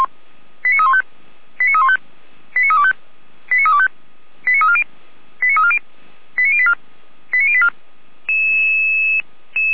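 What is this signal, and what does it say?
Five-tone selective call (ZVEI), heard as quick runs of five tones, tones of about 70 ms. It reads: 89315 89315 89325 89325 89340 89340 89084 89084
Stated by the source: ZVEI five-tone selective call 89313 received on an SDR.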